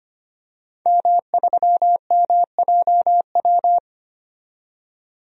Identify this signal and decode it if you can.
Morse code, heard as a keyed tone, about 25 wpm, 700 Hz. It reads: M3MJW
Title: DXCC-100 Call Signs - 5 Characters - 25wpm